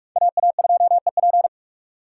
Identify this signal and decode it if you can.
Morse code, heard as a keyed tone, about 43 wpm, 700 Hz. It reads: AA2EP